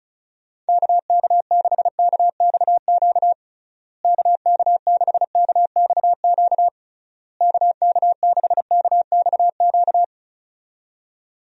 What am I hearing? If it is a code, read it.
Morse code, heard as a keyed tone, about 34 wpm, 700 Hz. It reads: KK6KXQ KK6KXQ KK6KXQ
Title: USA Call Signs - 6 Characters - Repeated 3x - 35wpm